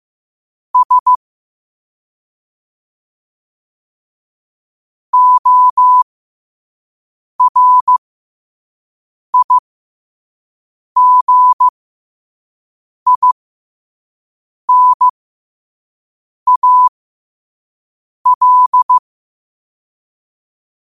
Morse code, heard as a keyed tone, about 15 wpm, spaced as for 5 wpm, 1000 Hz. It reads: S ORIGINAL
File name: qotdfiles / 5wpm